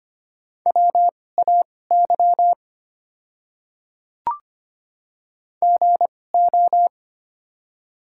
Morse code, heard as a keyed tone, about 25 wpm, spaced as for 18 wpm, 700 Hz. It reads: WAY GO
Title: 2x Character Spacing - Top 100 Words - 25wpm